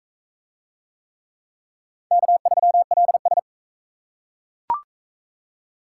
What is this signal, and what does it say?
Morse code, heard as a keyed tone, about 36 wpm, 700 Hz. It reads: K3LS